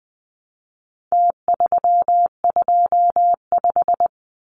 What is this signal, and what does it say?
Morse code, heard as a keyed tone, about 20 wpm, 700 Hz. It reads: T325
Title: Numbers - Contest Style - 4-Digit 20wpm